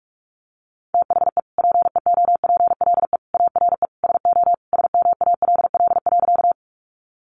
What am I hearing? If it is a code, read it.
Morse code, heard as a keyed tone, about 44 wpm, 700 Hz. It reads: THE PEOPLE ARE SO SMALL.
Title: Additional Easy Phrases 45wpm